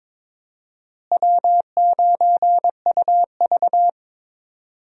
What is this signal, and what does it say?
Morse code, heard as a keyed tone, about 21 wpm, 700 Hz. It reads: W9UV
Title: USA Call Signs - 4 Characters - Repeated 2x - 22wpm